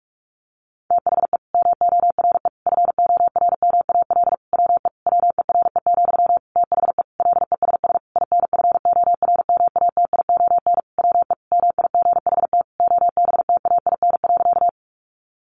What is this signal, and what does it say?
Morse code, heard as a keyed tone, about 44 wpm, 700 Hz. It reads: THE MORE FORMAL WE WERE, THE LESS INFORMATION WE MIGHT OBTAIN.
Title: Sentences from Top 1300 Words 45wpm